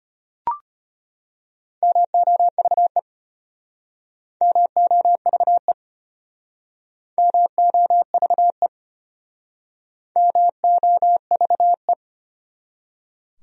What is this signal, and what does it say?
Morse code, heard as a keyed tone, about 32 wpm, 700 Hz. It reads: MOVE MOVE MOVE MOVE